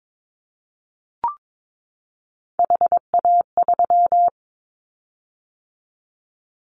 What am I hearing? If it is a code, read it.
Morse code, heard as a keyed tone, about 22 wpm, 700 Hz. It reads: HA3
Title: Callsign Prefixes for Top 50 DXCC - 22wpm